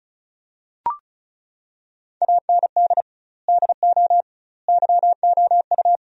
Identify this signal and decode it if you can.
Morse code, heard as a keyed tone, about 36 wpm, 700 Hz. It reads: AND DO YOU
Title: 2x Word-Spacing - Sentences from Top 100 Words - 35wpm